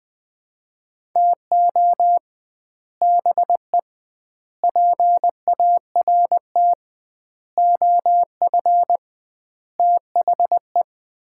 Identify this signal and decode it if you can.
Morse code, heard as a keyed tone, about 20 wpm, 700 Hz. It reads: TO BE PART OF THE
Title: 2x Word-Spacing - Sets of 5 Words - 20wpm